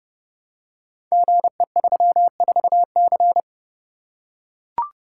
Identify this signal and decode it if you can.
Morse code, heard as a keyed tone, about 29 wpm, 700 Hz. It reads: GE34C
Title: Code Groups - 5 Letter:Numbers - 30wpm